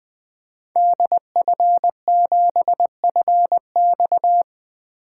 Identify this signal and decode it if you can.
Morse code, heard as a keyed tone, about 20 wpm, 700 Hz. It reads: DF7FX
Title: 2023 SOTA Activator Call Signs - 20wpm